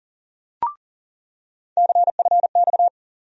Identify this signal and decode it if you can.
Morse code, heard as a keyed tone, about 40 wpm, 700 Hz. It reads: CFX
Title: USA Call Sign Postfixes - 3 Letters 40wpm